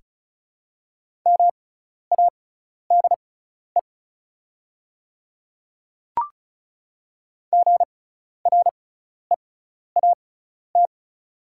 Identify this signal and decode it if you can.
Morse code, heard as a keyed tone, about 36 wpm, spaced as for 12 wpm, 690 Hz. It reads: MADE GREAT
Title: 6x Character Spacing - Top 200 Words - 35wpm